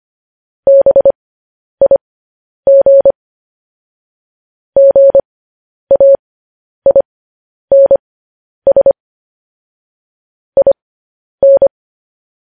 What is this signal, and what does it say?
Morse code, heard as a keyed tone, about 25 wpm, spaced as for 10 wpm, 550 Hz. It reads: BIG GAINS IN
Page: news-headlines-in-morse-code-at-10-wpm